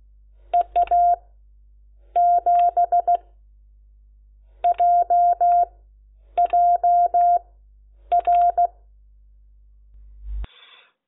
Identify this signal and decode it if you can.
Morse code, heard as a keyed tone, about 16 wpm, spaced as for 8 wpm, 700 Hz. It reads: EA7JJR
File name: Lecciones / CLASE Nº10